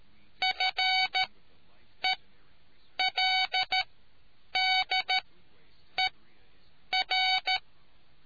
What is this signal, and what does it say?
Morse code, heard as a keyed tone, about 13 wpm, spaced as for 8 wpm, 750 Hz. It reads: FELDER